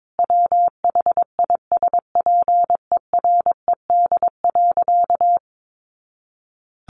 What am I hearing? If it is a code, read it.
Morse code, heard as a keyed tone, about 22 wpm, 700 Hz. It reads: WHISPERED.
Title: A Sound Of Thunder 22wpm